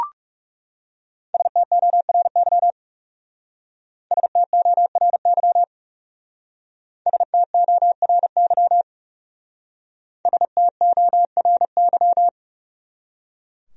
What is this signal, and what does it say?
Morse code, heard as a keyed tone, about 38 wpm, 700 Hz. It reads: STORY STORY STORY STORY